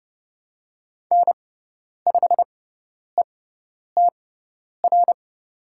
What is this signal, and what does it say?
Morse code, heard as a keyed tone, about 30 wpm, spaced as for 10 wpm, 700 Hz. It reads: N5ETR
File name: Lesson 16g - Letter c - Call Signs 30wpm